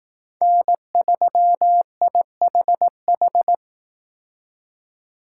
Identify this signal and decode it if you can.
Morse code, heard as a keyed tone, about 18 wpm, 700 Hz. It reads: N3IHH
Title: Speed Racing - USA 1x3 Call Signs 15wpm